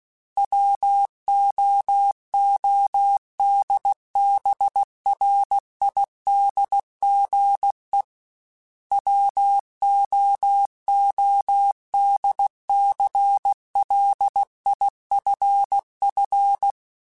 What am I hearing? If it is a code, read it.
Morse code, heard as a keyed tone, about 16 wpm, 800 Hz. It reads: WOODBRIDGE WOODCLIFF
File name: fromSMs / FromSMs mp3s / Cities@15